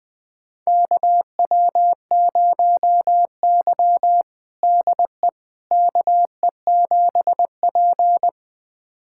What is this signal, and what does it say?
Morse code, heard as a keyed tone, about 20 wpm, 700 Hz. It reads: KW0Y DE KE7P
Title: USA Callsign DE Callsign - 4 Characters - 20wpm